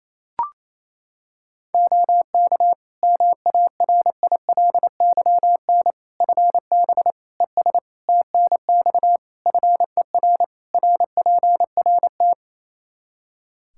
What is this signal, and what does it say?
Morse code, heard as a keyed tone, about 28 wpm, 700 Hz. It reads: OK MARILYN FB ES TNX FER RPRT
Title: Practice QSO - Exchange 2 - Info - 28wpm